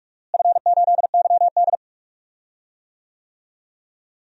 Morse code, heard as a keyed tone, about 46 wpm, 700 Hz. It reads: F8YD
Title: DXCC-100 Call Signs - 4 Characters - Repeated 2x - 45wpm